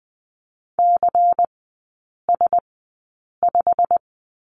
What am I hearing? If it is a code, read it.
Morse code, heard as a keyed tone, about 20 wpm, spaced as for 8 wpm, 700 Hz. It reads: CS5
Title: 178 Lesson 29e - Slash - Three Characters 20wpm